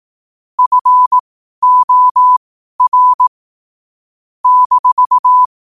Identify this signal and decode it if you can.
Morse code, heard as a keyed tone, about 18 wpm, spaced as for 12 wpm, 1000 Hz. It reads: FOR -
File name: qotdfiles / 12wpm